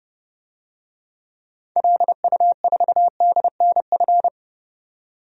Morse code, heard as a keyed tone, about 30 wpm, 700 Hz. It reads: LU4DNF
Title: DXCC-100 Call Signs - 6 Characters - 30wpm